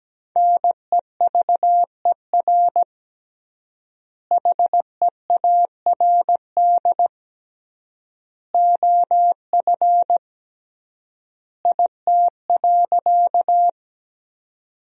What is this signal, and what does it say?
Morse code, heard as a keyed tone, about 17 wpm, 700 Hz. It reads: NEVER HEARD OF IT.
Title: 3x Word-Spacing - Sentences from Top 400 Words - 17wpm